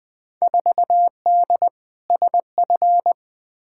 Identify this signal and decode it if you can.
Morse code, heard as a keyed tone, about 20 wpm, 700 Hz. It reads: 4D SF
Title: Field Day Exchanges Encore - 20wpm